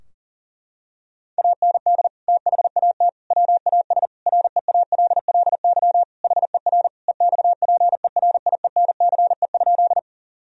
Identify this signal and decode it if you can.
Morse code, heard as a keyed tone, about 40 wpm, 700 Hz. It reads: AND THAT WAS REALLY HER EXPERIENCE?
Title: Sentences from Top 1000 Words - Review Part 3 - 40wpm